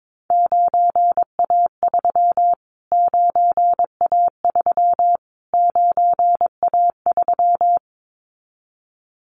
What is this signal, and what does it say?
Morse code, heard as a keyed tone, about 22 wpm, 700 Hz. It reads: 9A3 9A3 9A3
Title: Callsign Prefixes for Top 25 DXCC - Repeated 3x - 22wpm